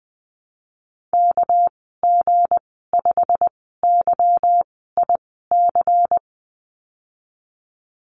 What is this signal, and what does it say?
Morse code, heard as a keyed tone, about 20 wpm, spaced as for 15 wpm, 700 Hz. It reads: KG5YIC
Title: 200 Lesson 32g - Letter k - Call Signs 20wpm